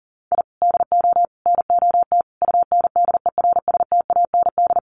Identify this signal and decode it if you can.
Morse code, heard as a keyed tone, about 40 wpm, 700 Hz. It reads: I DO NOT UNDERSTAND
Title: Sets of 5 Words Encore 40wpm